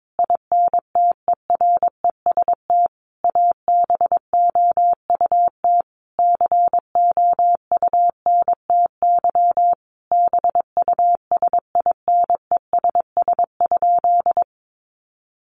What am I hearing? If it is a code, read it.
Morse code, heard as a keyed tone, about 22 wpm, 700 Hz. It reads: INTEREST ABOUT COUNTY BUSINESS?
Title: Sentences from Top 1200 Words 22wpm